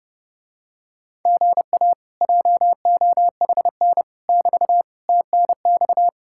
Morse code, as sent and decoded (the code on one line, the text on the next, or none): --. .- / .--- --- .... -. / -...- / - -. -..-
GA JOHN <BT> TNX